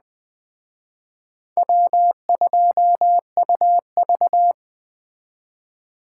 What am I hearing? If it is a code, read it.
Morse code, heard as a keyed tone, about 24 wpm, 700 Hz. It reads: W2UV
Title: Speed Racing - USA 1x2 Call Signs 20wpm